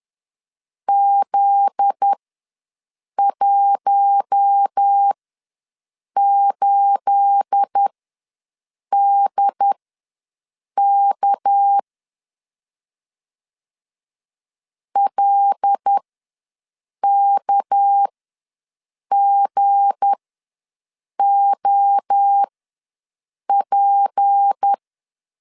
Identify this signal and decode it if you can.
Morse code, heard as a keyed tone, about 10 wpm, spaced as for 6 wpm, 800 Hz. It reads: Z18DK LKGOP
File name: LEZIONI / 9 28 ws 10wpm 4124